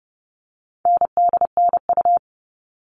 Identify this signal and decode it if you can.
Morse code, heard as a keyed tone, about 30 wpm, 700 Hz. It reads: NDNU
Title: All Cut Numbers - 4-Digit 30wpm